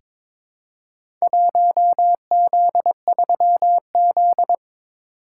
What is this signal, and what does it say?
Morse code, heard as a keyed tone, about 22 wpm, 700 Hz. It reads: 1Z3Z
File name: practice / 22wpm / Four Letter-Number Combinations - 22wpm